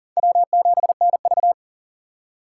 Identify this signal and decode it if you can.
Morse code, heard as a keyed tone, about 40 wpm, 700 Hz. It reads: W7NV
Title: USA Call Signs - 6x Character Spacing - 40wpm